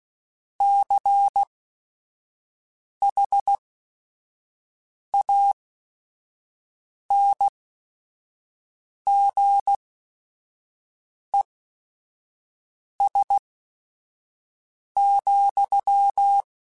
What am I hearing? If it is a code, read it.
Morse code, heard as a keyed tone, about 16 wpm, spaced as for 5 wpm, 800 Hz. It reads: CHANGES,
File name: Technical Subjects / Technical Subjects-mp3s / Reactance @5